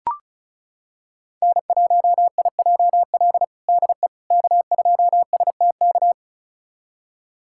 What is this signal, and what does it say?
Morse code, heard as a keyed tone, about 36 wpm, 700 Hz. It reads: N1IJL DE K2STK
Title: USA Callsign DE Callsign - 5 Characters - 35wpm